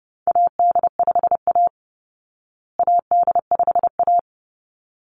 Morse code, heard as a keyed tone, about 30 wpm, 700 Hz. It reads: AD5A AD5A
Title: Call Signs Repeated 3x - 30wpm